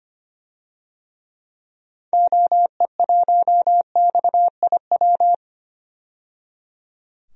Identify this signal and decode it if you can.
Morse code, heard as a keyed tone, about 25 wpm, 700 Hz. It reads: OE1XIW
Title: DXCC-100 Call Signs - 6 Characters - 25wpm